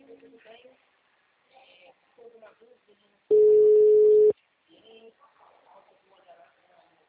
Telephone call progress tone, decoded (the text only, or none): ringback tone